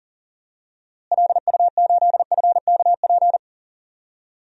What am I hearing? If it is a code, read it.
Morse code, heard as a keyed tone, about 40 wpm, 700 Hz. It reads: LU8FKP
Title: DXCC-100 Call Signs - 6 Characters - Repeated 3x - 40wpm